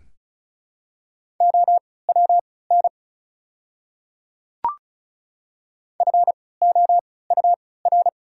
Morse code, heard as a keyed tone, about 36 wpm, spaced as for 20 wpm, 700 Hz. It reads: OWN FOUR
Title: 3x Character Spacing - Top 500 Words - 35wpm